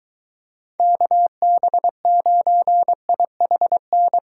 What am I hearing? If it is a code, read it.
Morse code, heard as a keyed tone, about 23 wpm, 700 Hz. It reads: KB9IHN